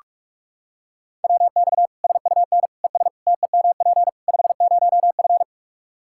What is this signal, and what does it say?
Morse code, heard as a keyed tone, about 46 wpm, 700 Hz. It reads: WX SUN ES TEMP 50F
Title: Practice QSO - Exchange 2 - Info - 45wpm